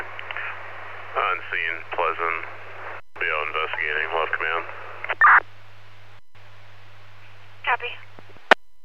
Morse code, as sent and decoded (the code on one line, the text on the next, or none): none